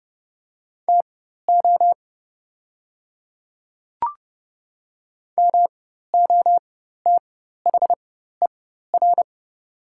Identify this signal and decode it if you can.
Morse code, heard as a keyed tone, about 30 wpm, spaced as for 14 wpm, 700 Hz. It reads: TO MOTHER
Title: 4x Character Spacing - Top 300 Words - 30wpm